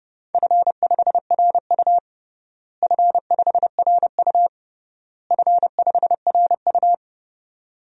Morse code, heard as a keyed tone, about 31 wpm, 700 Hz. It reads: F5RU F5RU F5RU
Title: DXCC-100 Call Signs - 4 Characters - Repeated 3x - 30wpm